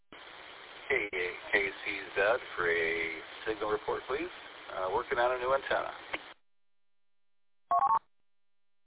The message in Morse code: none

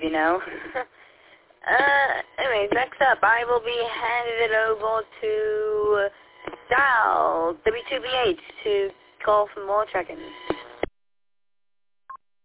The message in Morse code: none

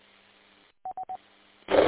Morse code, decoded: none

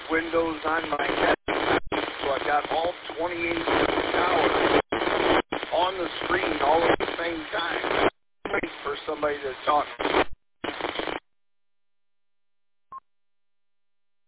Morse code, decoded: none

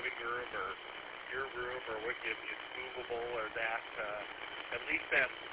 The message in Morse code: none

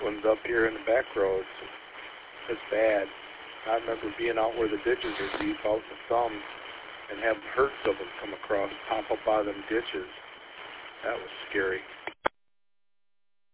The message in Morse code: none

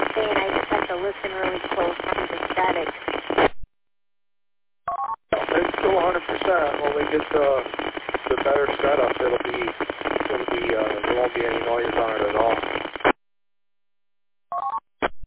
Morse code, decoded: none

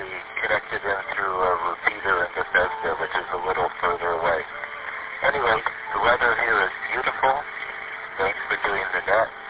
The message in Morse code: none

